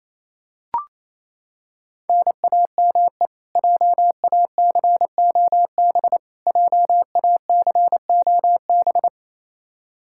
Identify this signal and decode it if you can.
Morse code, heard as a keyed tone, about 28 wpm, 700 Hz. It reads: NAME JACOB JACOB
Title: K1USN SST QSO Practice 28wpm